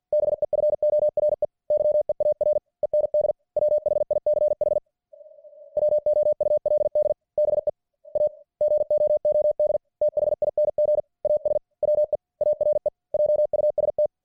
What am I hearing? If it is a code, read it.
Morse code, heard as a keyed tone, about 50 wpm, 600 Hz. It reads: BEFORE YEAR END WHICH WOULD BE A GOOD THING AS WE ARE JUST